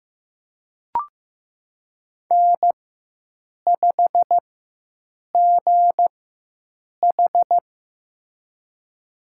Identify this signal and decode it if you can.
Morse code, heard as a keyed tone, about 15 wpm, spaced as for 7 wpm, 700 Hz. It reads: N5GH
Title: USA Call Signs - 4x Character Spacing - 15wpm